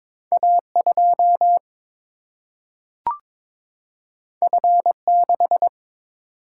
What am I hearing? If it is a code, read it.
Morse code, heard as a keyed tone, about 22 wpm, 700 Hz. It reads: A2 F6
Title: Single-Letter Single-Number 22wpm